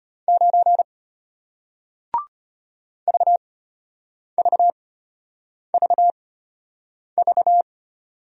Morse code, heard as a keyed tone, about 33 wpm, spaced as for 8 wpm, 700 Hz. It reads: 9 VVVV